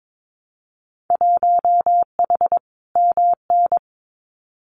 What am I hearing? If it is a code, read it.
Morse code, heard as a keyed tone, about 22 wpm, 700 Hz. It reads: 1H MN
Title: Winter Field Day Exchanges Encore 22wpm